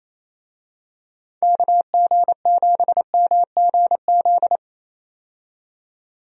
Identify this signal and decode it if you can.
Morse code, heard as a keyed tone, about 27 wpm, 700 Hz. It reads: KG7MGZ
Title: USA Call Signs - 6 Characters - 28wpm